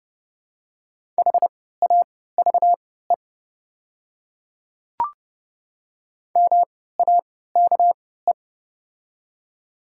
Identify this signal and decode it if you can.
Morse code, heard as a keyed tone, about 32 wpm, spaced as for 17 wpm, 700 Hz. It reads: HAVE MAKE
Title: 3x Character Spacing - Top 300 Words - 30wpm